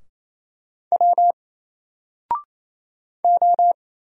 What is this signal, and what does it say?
Morse code, heard as a keyed tone, about 27 wpm, 700 Hz. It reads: W O
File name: Single-Letter-Number-28wpm